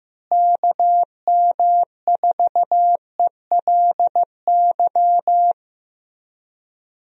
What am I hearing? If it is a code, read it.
Morse code, heard as a keyed tone, about 14 wpm, 700 Hz. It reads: KM4ELY
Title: USA Call Signs - 6 Characters - Repeated 2x - 15wpm